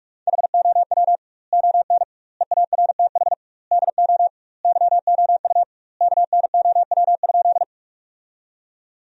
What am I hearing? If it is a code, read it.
Morse code, heard as a keyed tone, about 41 wpm, 700 Hz. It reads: HOW ON EARTH DO YOU KNOW?
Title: 2x Word-Spacing - Sentences from Top 200 Words - 45wpm